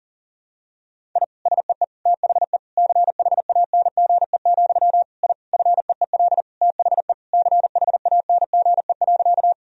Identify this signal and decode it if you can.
Morse code, heard as a keyed tone, about 37 wpm, 700 Hz. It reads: I SEE THE CHANGE, I FEEL THE CHANGE.
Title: Sentences from Top 1000 Words - Review Part 1 - 40wpm